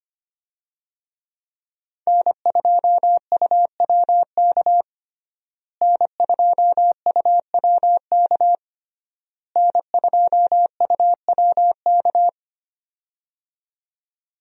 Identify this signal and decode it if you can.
Morse code, heard as a keyed tone, about 25 wpm, 700 Hz. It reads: N2UWK N2UWK N2UWK
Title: USA Call Signs - 5 Characters - Repeated 3x - 25wpm